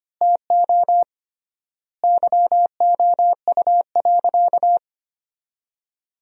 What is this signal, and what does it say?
Morse code, heard as a keyed tone, about 25 wpm, 700 Hz. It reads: TO YOU.
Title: 3x Word-Spacing - Sentences from Top 500 Words - 25wpm